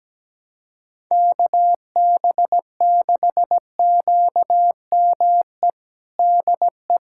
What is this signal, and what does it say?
Morse code, heard as a keyed tone, about 17 wpm, 700 Hz. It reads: KB6QME DE
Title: USA Callsign DE Callsign - 6 Characters - 17wpm